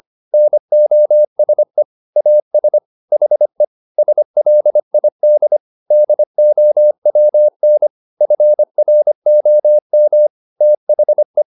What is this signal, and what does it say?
Morse code, heard as a keyed tone, about 25 wpm, 600 Hz. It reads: NOSE AS HE SLID DOWN FROM THE